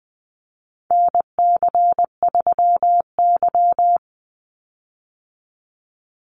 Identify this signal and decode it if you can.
Morse code, heard as a keyed tone, about 20 wpm, 700 Hz. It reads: NC3Y
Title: Call Signs - 20wpm